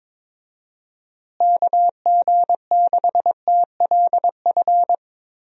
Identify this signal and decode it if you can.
Morse code, heard as a keyed tone, about 22 wpm, 700 Hz. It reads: KG6TLF
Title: USA Call Signs - 6 Characters - Repeated 3x - 22wpm